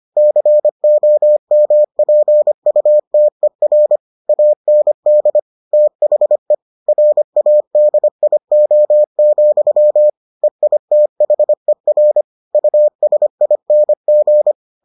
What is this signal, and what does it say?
Morse code, heard as a keyed tone, about 25 wpm, 600 Hz. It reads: COMPUTER AND THE RADIO, EITHER USING